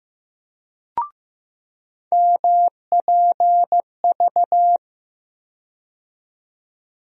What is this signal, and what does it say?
Morse code, heard as a keyed tone, about 15 wpm, 700 Hz. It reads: MPV
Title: USA Call Sign Postfixes - 3 Letters 15wpm